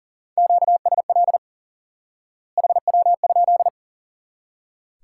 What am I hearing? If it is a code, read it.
Morse code, heard as a keyed tone, about 40 wpm, 700 Hz. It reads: QSL HW?
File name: IQE Course 3 - Lesson 89 - YAGI - Two Elements - 40wpm